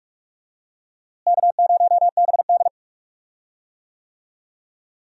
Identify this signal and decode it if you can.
Morse code, heard as a keyed tone, about 42 wpm, 700 Hz. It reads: K0BD
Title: USA Call Signs - 4 Characters - Repeated 2x - 45wpm